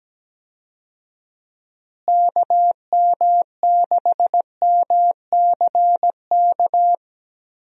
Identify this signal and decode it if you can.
Morse code, heard as a keyed tone, about 17 wpm, 700 Hz. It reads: KM6MCK
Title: USA Call Signs - 6 Characters - Repeated 2x - 17wpm